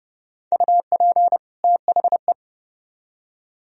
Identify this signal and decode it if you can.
Morse code, heard as a keyed tone, about 37 wpm, 700 Hz. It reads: UP THE
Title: Speed-Racing - Sets of 2 Words 30wpm